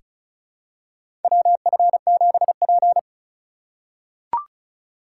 Morse code, heard as a keyed tone, about 36 wpm, 700 Hz. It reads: WF7P
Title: USA Call Signs - 4 Characters - 3x Character Spacing - 35wpm